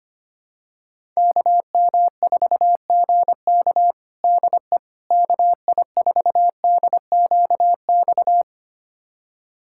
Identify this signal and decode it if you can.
Morse code, heard as a keyed tone, about 25 wpm, 700 Hz. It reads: KM4GK DE KI4DQX
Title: USA Callsign DE Callsign 25wpm